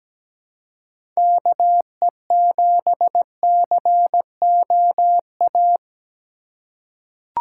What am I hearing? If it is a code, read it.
Morse code, heard as a keyed tone, about 17 wpm, 700 Hz. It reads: KE7COA
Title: USA Call Signs - 6 Characters - 4x Character Spacing - 17wpm